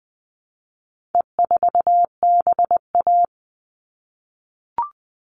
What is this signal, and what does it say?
Morse code, heard as a keyed tone, about 20 wpm, 700 Hz. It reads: E4BA
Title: All Cut Numbers - 4-Digit 20wpm